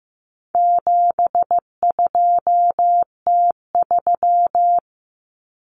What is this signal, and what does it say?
Morse code, heard as a keyed tone, about 15 wpm, 700 Hz. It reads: 72T3
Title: Common Cut Numbers - 4-Digit 15wpm